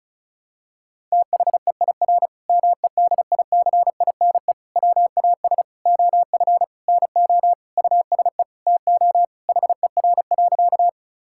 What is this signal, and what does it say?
Morse code, heard as a keyed tone, about 36 wpm, 700 Hz. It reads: THEIR MEDICINE WAS OF NO USE TO HER.
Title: Sentences from Top 1000 Words - Review Part 6 - 35wpm